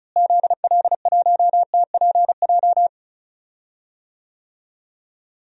Morse code, heard as a keyed tone, about 36 wpm, 700 Hz. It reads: ZL1TPJ
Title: DXCC-100 Call Signs - 6 Characters - Repeated 3x - 35wpm